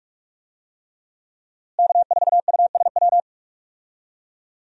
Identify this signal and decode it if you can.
Morse code, heard as a keyed tone, about 42 wpm, 700 Hz. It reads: K4USW
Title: USA Call Signs - 5 Characters - Repeated 3x - 45wpm